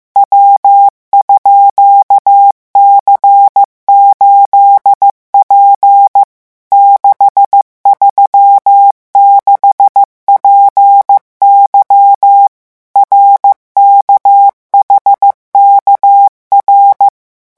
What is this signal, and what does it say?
Morse code, heard as a keyed tone, about 15 wpm, 800 Hz. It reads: W_C8P 636PY RKHKR
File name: morse mp3 / revision12-vitesse 14 mots